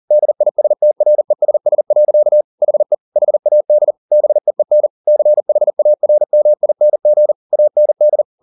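Morse code, heard as a keyed tone, about 40 wpm, 600 Hz. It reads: DISTRESS. HE HAD BEEN CHARMING AND
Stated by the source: Morse code at 40 wpm, practice text, CW med Gnister.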